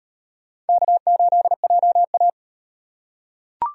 Morse code, heard as a keyed tone, about 35 wpm, 700 Hz. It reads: K8JA